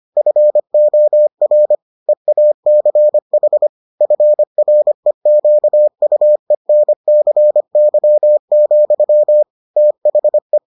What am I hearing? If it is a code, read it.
Morse code, heard as a keyed tone, about 25 wpm, 600 Hz. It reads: FOR EACH FREQUENCY, THE